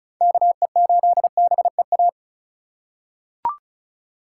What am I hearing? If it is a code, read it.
Morse code, heard as a keyed tone, about 34 wpm, 700 Hz. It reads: KE8BEA
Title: USA Call Signs - 6 Characters - 4x Character Spacing - 35wpm